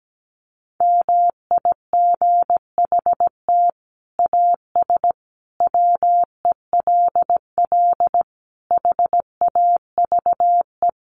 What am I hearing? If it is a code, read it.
Morse code, heard as a keyed tone, about 17 wpm, 700 Hz. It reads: MIGHT AS WELL HAVE
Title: Sets of 5 Words Encore2 17wpm